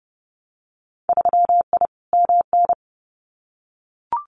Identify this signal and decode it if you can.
Morse code, heard as a keyed tone, about 30 wpm, 700 Hz. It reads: 3I MN
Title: Winter Field Day Exchanges 30wpm